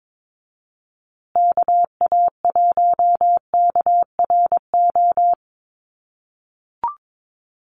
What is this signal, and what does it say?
Morse code, heard as a keyed tone, about 22 wpm, 700 Hz. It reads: KA1KRO
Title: USA Call Signs Repeated 2x 22wpm